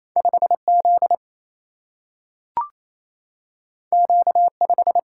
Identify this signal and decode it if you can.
Morse code, heard as a keyed tone, about 28 wpm, 700 Hz. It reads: 5 Z Q 5
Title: Single-Letter Single-Number 28wpm